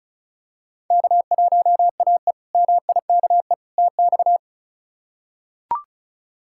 Morse code, heard as a keyed tone, about 36 wpm, 700 Hz. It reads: K1AE MIKE TX
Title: K1USN SST QSO Practice - 35wpm